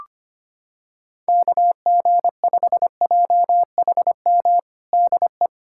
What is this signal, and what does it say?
Morse code, heard as a keyed tone, about 25 wpm, 700 Hz. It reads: KG5JHM DE
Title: USA Callsign DE Callsign - 6 Characters - 25wpm